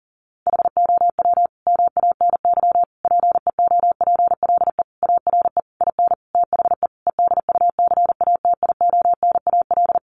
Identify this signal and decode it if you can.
Morse code, heard as a keyed tone, about 40 wpm, 700 Hz. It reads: HOW MANY PEOPLE ARE IN THE EDUCATIONAL
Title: Sentences from Top 1200 Words 40wpm